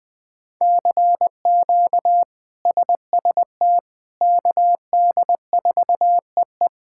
Morse code, heard as a keyed tone, about 21 wpm, 700 Hz. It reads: CQ SST KD4EE
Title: K1USN SST QSO Practice - 20wpm